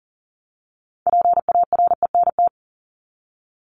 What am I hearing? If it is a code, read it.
Morse code, heard as a keyed tone, about 40 wpm, 700 Hz. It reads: PARENT
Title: Top 1751-2000 Words 40wpm